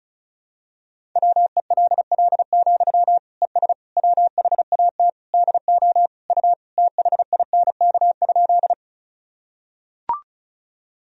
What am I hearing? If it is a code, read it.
Morse code, heard as a keyed tone, about 36 wpm, 700 Hz. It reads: WELL, ES WHAT DO U THINK?
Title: Sentences from Top 200 Words and Abbreviations 35wpm